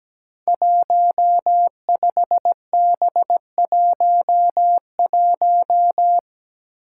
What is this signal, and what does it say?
Morse code, heard as a keyed tone, about 17 wpm, 700 Hz. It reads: 15B11
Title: Five Challenging Letter-Number Combinations 17wpm